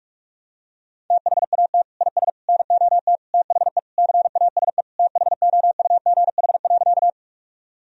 Morse code, heard as a keyed tone, about 46 wpm, 700 Hz. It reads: THAT IS NOT THE CASE THOUGH.
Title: Sentences from Top 1000 Words - Review Part 2 - 45wpm